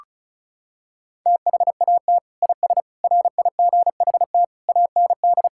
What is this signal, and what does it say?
Morse code, heard as a keyed tone, about 34 wpm, 700 Hz. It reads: THAT IS RIGHT AND
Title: Sentences from Top 1000 Words - Review Part 5 - 35wpm